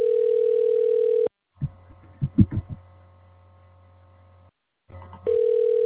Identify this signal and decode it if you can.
Ringback tone (North America), 440 + 480 Hz.